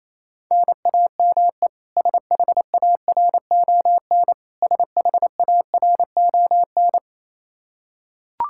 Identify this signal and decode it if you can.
Morse code, heard as a keyed tone, about 28 wpm, 700 Hz. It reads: NAME SHARON SHARON
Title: Practice QSO - Exchange 1 - Beginning - 28wpm